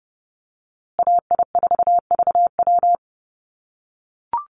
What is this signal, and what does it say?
Morse code, heard as a keyed tone, about 30 wpm, 700 Hz. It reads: AI4VW
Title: USA Call Signs Repeated 3x 30wpm